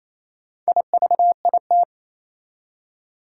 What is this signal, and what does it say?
Morse code, heard as a keyed tone, about 28 wpm, 700 Hz. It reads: IVIT
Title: Four Challenging Letter-Number Combinations 28wpm